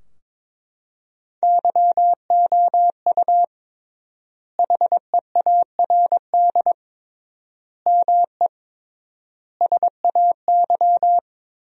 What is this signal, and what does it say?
Morse code, heard as a keyed tone, about 22 wpm, 700 Hz. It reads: YOU HEARD ME SAY
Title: 3x Word-Spacing - Sentences from Top 500 Words - 22wpm